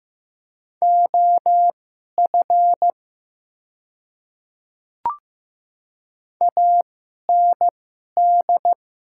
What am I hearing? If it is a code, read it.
Morse code, heard as a keyed tone, about 15 wpm, spaced as for 11 wpm, 700 Hz. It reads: OF AND
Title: 2x Character Spacing - Top 100 Words - 15wpm